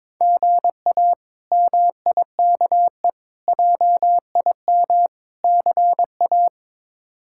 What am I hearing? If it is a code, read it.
Morse code, heard as a keyed tone, about 22 wpm, 700 Hz. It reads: GA MIKE JIM CA
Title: K1USN SST QSO Practice - 22wpm